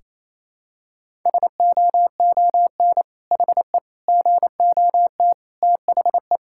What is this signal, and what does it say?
Morse code, heard as a keyed tone, about 28 wpm, 700 Hz. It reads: SOON HE GOT THE
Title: Sentences from Top 1600 Words 28wpm